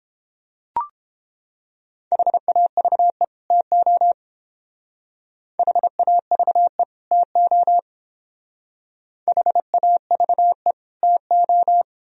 Morse code, heard as a keyed tone, about 28 wpm, 700 Hz. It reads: HAVE TO HAVE TO HAVE TO